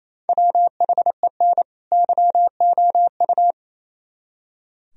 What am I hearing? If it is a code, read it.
Morse code, heard as a keyed tone, about 34 wpm, 700 Hz. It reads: WHEN YOU
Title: Speed-Racing - Sets of 2 Words 28wpm